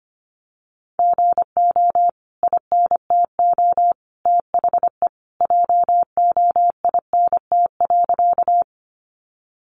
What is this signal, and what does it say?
Morse code, heard as a keyed tone, about 25 wpm, 700 Hz. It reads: GO INTO THE JOINT.
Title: Sentences from Top 1300 Words 25wpm